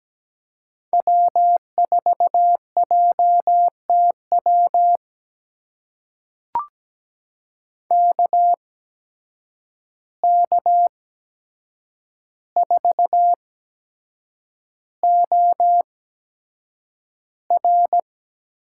Morse code, heard as a keyed tone, about 17 wpm, spaced as for 7 wpm, 700 Hz. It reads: W4JTW KK4OR
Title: USA Call Signs - 8x Character Spacing - 17wpm